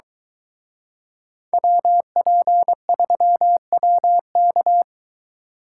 Morse code, heard as a keyed tone, about 23 wpm, 700 Hz. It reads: WP3WK